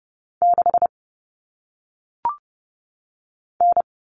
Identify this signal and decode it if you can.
Morse code, heard as a keyed tone, about 30 wpm, 700 Hz. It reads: 6 N
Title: Common Cut Numbers - 1-Digit 30wpm